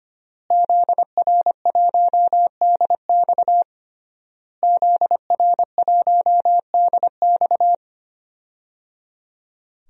Morse code, heard as a keyed tone, about 25 wpm, 700 Hz. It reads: ZR1DX ZR1DX
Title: DXCC-100 Call Signs - 5 Characters - Repeated 3x - 25wpm